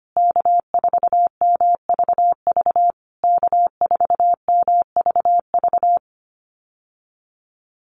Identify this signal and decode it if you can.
Morse code, heard as a keyed tone, about 25 wpm, 700 Hz. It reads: K4MVV K4MVV
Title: USA Call Signs Repeated 3x 25wpm